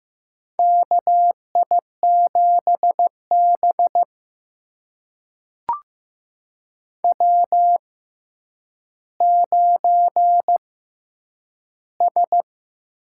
Morse code, heard as a keyed tone, about 15 wpm, spaced as for 9 wpm, 700 Hz. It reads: KI7B W9S
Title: USA Call Signs - 4 Characters - 6x Character Spacing - 15wpm